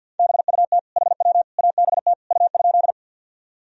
Morse code, heard as a keyed tone, about 52 wpm, 700 Hz. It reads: BUT HW ABT U?
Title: Sentences from Top 100 Words and Abbreviations 50wpm